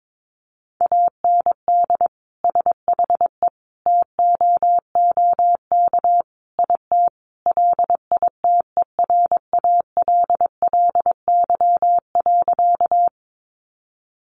Morse code, heard as a keyed tone, about 22 wpm, 700 Hz. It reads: AND SHE TOOK IT LITERALLY.
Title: Sentences from Top 1400 Words 22wpm